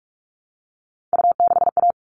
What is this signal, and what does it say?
Morse code, heard as a keyed tone, about 44 wpm, 700 Hz. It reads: U6A
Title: All Cut Numbers - 3-Digit 45wpm